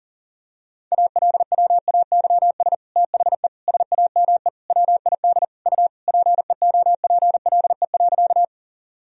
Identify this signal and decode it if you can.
Morse code, heard as a keyed tone, about 40 wpm, 700 Hz. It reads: ALWAYS THE SAME WID U PEOPLE.
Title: Sentences from Top 300 Words and Abbreviations 40wpm